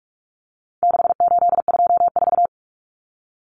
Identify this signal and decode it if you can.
Morse code, heard as a keyed tone, about 44 wpm, 700 Hz. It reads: 6824
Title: Common Cut Numbers - 4-Digit 45wpm